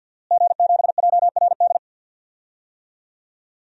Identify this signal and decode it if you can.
Morse code, heard as a keyed tone, about 53 wpm, 700 Hz. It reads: G7JRD
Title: DXCC-100 Call Signs - 5 Characters - Repeated 3x - 50wpm